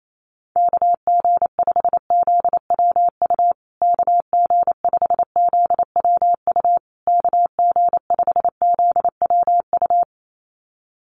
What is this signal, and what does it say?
Morse code, heard as a keyed tone, about 28 wpm, 700 Hz. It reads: KG5ZWU KG5ZWU KG5ZWU
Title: USA Call Signs Repeated 3x 28wpm